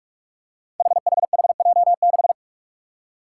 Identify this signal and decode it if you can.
Morse code, heard as a keyed tone, about 41 wpm, 700 Hz. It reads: HHHJ6
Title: Five Challenging Letter-Number Combinations 45wpm